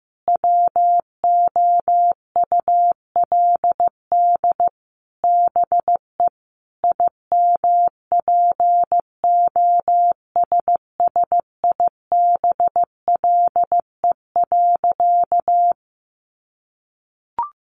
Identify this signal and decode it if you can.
Morse code, heard as a keyed tone, about 15 wpm, 700 Hz. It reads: WOULD BE IMPOSSIBLE.
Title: Sentences from Top 1100 Words 15wpm